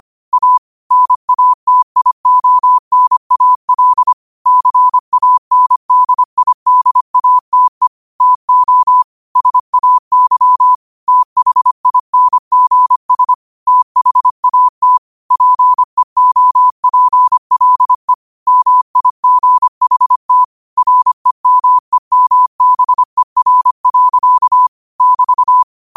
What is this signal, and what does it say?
Morse code, heard as a keyed tone, about 25 wpm, 1000 Hz. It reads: A NATIONAL CANDIDATE TO SAY THINGS THAT PEOPLE MIGHT REMEMBER. <BT>
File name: qotdfiles / 25wpm